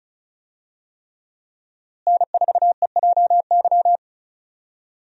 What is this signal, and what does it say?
Morse code, heard as a keyed tone, about 36 wpm, 700 Hz. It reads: N4EJY
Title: USA Call Signs - 5 Characters - Repeated 3x - 35wpm